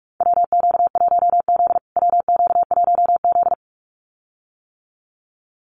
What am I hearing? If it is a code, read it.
Morse code, heard as a keyed tone, about 44 wpm, 700 Hz. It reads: WQ1Z WQ1Z
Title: USA Call Signs Repeated 2x 45wpm